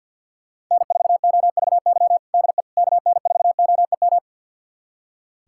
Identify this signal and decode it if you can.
Morse code, heard as a keyed tone, about 46 wpm, 700 Hz. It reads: N4OVY DE KN4OEM
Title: USA Callsign DE Callsign 50wpm